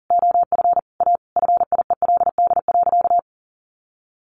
Morse code, heard as a keyed tone, about 40 wpm, 700 Hz. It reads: OF A FIELD.
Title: Sentences from Top 1000 Words 40wpm